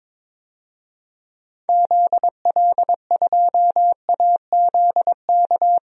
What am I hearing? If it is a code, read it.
Morse code, heard as a keyed tone, about 22 wpm, 700 Hz. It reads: ZL2AZK
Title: DXCC-100 Call Signs - 6 Characters - 22wpm